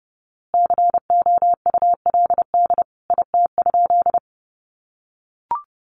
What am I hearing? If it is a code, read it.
Morse code, heard as a keyed tone, about 30 wpm, 700 Hz. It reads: COULD IT?
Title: A Sound Of Thunder 30wpm